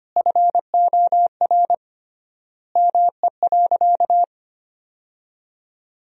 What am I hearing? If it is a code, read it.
Morse code, heard as a keyed tone, about 25 wpm, 700 Hz. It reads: FOR ME.
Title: 3x Word-Spacing - Sentences from Top 300 Words - 25wpm